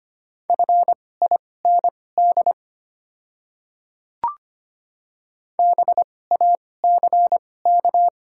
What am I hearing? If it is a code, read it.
Morse code, heard as a keyed tone, about 25 wpm, spaced as for 18 wpm, 700 Hz. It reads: FIND BACK
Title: 2x Character Spacing - Top 500 Words - 25wpm